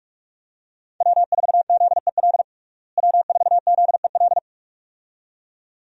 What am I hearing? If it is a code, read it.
Morse code, heard as a keyed tone, about 46 wpm, 700 Hz. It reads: W4ZEL W4ZEL
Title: USA Call Signs - 5 Characters - Repeated 2x - 45wpm